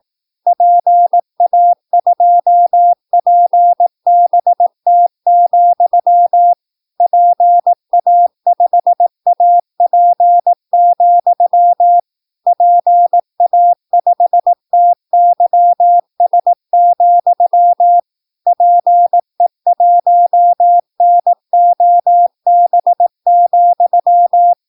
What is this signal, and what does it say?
Morse code, heard as a keyed tone, about 18 wpm, 700 Hz. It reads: PA2PBT, PA5AP, PA5TYS, PE1NOB,
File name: PI4AA Uitzending